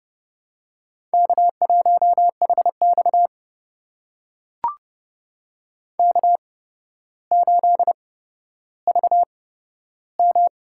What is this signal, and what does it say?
Morse code, heard as a keyed tone, about 30 wpm, spaced as for 13 wpm, 700 Hz. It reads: K1HX K8VM
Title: USA Call Signs - 4 Characters - 8x Character Spacing - 30wpm